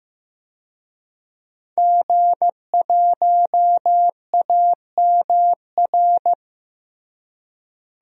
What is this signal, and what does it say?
Morse code, heard as a keyed tone, about 15 wpm, 700 Hz. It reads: G1AMR
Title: DXCC-100 Call Signs - 5 Characters - Repeated 3x - 15wpm